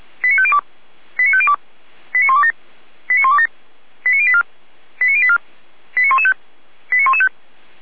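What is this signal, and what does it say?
Five-tone selective call (ZVEI), heard as quick runs of five tones, tones of about 70 ms. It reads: 89502 89502 89127 89127 89084 89084 89115 89115
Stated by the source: ZVEI five-tone call received by SDR.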